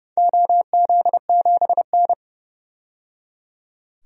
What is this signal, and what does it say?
Morse code, heard as a keyed tone, about 30 wpm, 700 Hz. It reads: OZ7N
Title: DXCC-100 Call Signs - 4 Characters - 30wpm